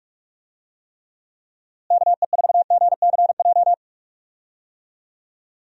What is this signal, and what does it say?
Morse code, heard as a keyed tone, about 47 wpm, 700 Hz. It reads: KE4GCJ
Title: USA Call Signs - 6 Characters - Repeated 3x - 45wpm